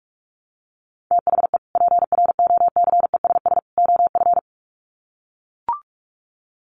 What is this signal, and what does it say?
Morse code, heard as a keyed tone, about 44 wpm, 700 Hz. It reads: THE PROCESS OF
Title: Sets of 3 Words 45wpm